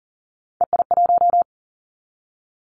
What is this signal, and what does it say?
Morse code, heard as a keyed tone, about 38 wpm, 700 Hz. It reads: EI1
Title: Callsign Prefixes for Top 50 DXCC - 40wpm